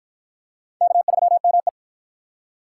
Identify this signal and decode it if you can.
Morse code, heard as a keyed tone, about 53 wpm, 700 Hz. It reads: K3ME